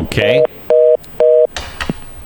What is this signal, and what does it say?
Telephone call progress tone (North America), heard as a telephone network's tone, about 480, 620 Hz, with speech. Reorder tone.